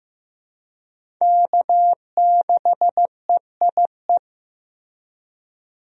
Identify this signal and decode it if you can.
Morse code, heard as a keyed tone, about 16 wpm, 700 Hz. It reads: K6EIE
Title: USA Call Signs - 5 Characters - Repeated 3x - 15wpm